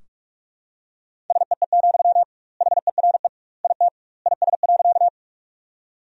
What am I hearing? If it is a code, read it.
Morse code, heard as a keyed tone, about 46 wpm, 700 Hz. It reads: SEE, HERE IT IS.
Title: 2x Word-Spacing - Sentences from Top 200 Words - 45wpm